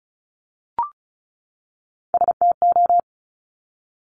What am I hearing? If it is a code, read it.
Morse code, heard as a keyed tone, about 36 wpm, 700 Hz. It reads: STO
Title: Top Three Letter Combinations in Words 35wpm